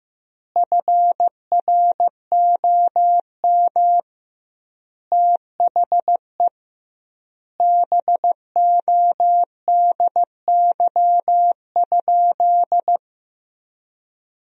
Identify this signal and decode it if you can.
Morse code, heard as a keyed tone, about 16 wpm, 700 Hz. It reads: FROM THE BODY?
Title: 2x Word-Spacing - Sentences from Top 400 Words - 15wpm